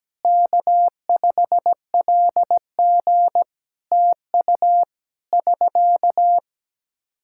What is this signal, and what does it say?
Morse code, heard as a keyed tone, about 17 wpm, 700 Hz. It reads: K5LG TU <SK>
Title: Practice QSO - Exchange 4 - Ending - 17wpm